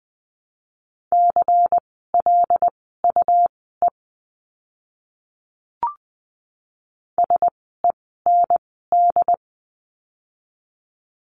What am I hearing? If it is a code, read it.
Morse code, heard as a keyed tone, about 20 wpm, spaced as for 15 wpm, 700 Hz. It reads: CLUE SEND
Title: 101 Lesson 17f - Letter u - Words 20wpm